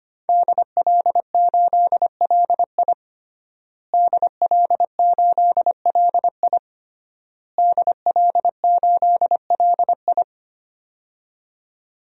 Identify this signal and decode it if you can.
Morse code, heard as a keyed tone, about 25 wpm, 700 Hz. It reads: DL8LI DL8LI DL8LI
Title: DXCC-100 Call Signs - 5 Characters - Repeated 3x - 25wpm